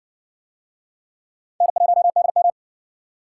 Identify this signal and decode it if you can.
Morse code, heard as a keyed tone, about 59 wpm, 700 Hz. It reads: N1DM